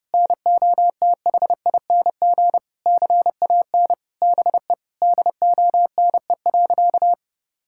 Morse code, heard as a keyed tone, about 32 wpm, 700 Hz. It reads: NOTHING CAN BE DONE.
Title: Sentences from Top 1600 Words 30wpm